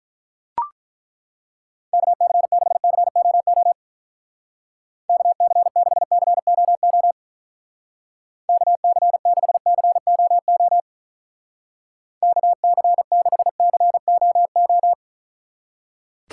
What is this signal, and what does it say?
Morse code, heard as a keyed tone, about 43 wpm, 700 Hz. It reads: KC6COO KC6COO KC6COO KC6COO
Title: Speed Racing - USA 2x3 Call Signs 35wpm